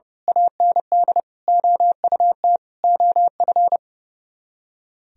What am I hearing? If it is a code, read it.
Morse code, heard as a keyed tone, about 28 wpm, 700 Hz. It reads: AND OUT OF
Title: Speed-Racing - Sets of 4 Words 30wpm